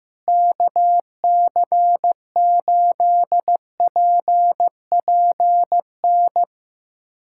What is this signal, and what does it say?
Morse code, heard as a keyed tone, about 15 wpm, 700 Hz. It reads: KC8PPN
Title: USA Callsign DE Callsign - 6 Characters - 15wpm